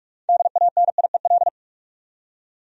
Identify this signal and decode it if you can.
Morse code, heard as a keyed tone, about 47 wpm, 700 Hz. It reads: DANIEL
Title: USA Common Names - 45wpm